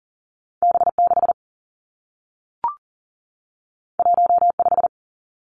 Morse code, heard as a keyed tone, about 38 wpm, 700 Hz. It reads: B6 15
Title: Two Challenging Letter-Number Combinations 40wpm